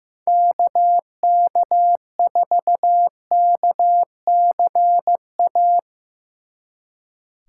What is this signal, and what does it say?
Morse code, heard as a keyed tone, about 15 wpm, 700 Hz. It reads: KK4KCA
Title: Speed Racing - USA Call Signs 15wpm